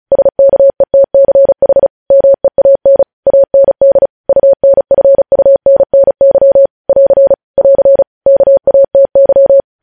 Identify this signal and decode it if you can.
Morse code, heard as a keyed tone, about 36 wpm, 550 Hz. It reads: SKETCH MEAN AND UNFUNNY <AR> <AR> KATY
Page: news headlines in morse code at 35 wpm